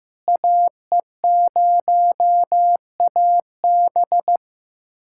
Morse code, heard as a keyed tone, about 16 wpm, 700 Hz. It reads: AE0AB
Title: USA Callsign DE Callsign - 5 Characters - 15wpm